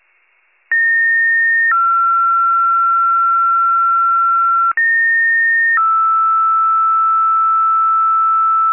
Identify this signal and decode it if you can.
Two-tone page, heard as a one-second tone, then a three-second tone, about 1850 then 1360 Hz, about 1850 then 1320 Hz. It is heard from 0.7 to 8.7 s.